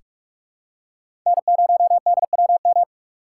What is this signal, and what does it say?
Morse code, heard as a keyed tone, about 46 wpm, 700 Hz. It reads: N0DWM
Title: USA Call Signs - 5 Characters - 4x Character Spacing - 45wpm